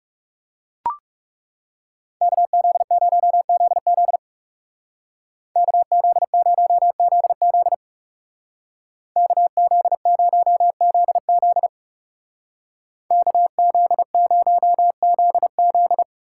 Morse code, heard as a keyed tone, about 36 wpm, 700 Hz. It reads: KZ0ZZ KZ0ZZ KZ0ZZ KZ0ZZ
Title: Speed Racing - USA 2x2 Call Signs 30wpm